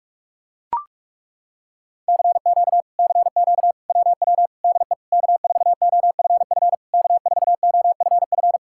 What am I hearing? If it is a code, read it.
Morse code, heard as a keyed tone, about 40 wpm, 700 Hz. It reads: CQ CQ WW DE K4OFF K4OFF